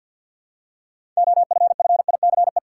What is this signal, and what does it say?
Morse code, heard as a keyed tone, about 53 wpm, 700 Hz. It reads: OFFICE
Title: Six Letter Words - 50wpm